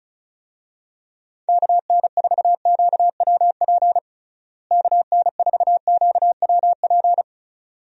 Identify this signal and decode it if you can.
Morse code, heard as a keyed tone, about 36 wpm, 700 Hz. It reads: KN4QWP KN4QWP
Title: USA Call Signs - 6 Characters - Repeated 3x - 35wpm